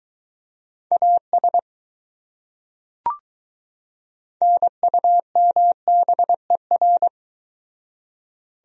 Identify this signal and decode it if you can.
Morse code, heard as a keyed tone, about 22 wpm, 700 Hz. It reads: AS NUMBER